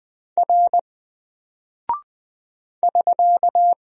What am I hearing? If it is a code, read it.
Morse code, heard as a keyed tone, about 20 wpm, 700 Hz. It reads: R <SK>
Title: Single Character (Letter:Number:Punctuation:Prosign) 20wpm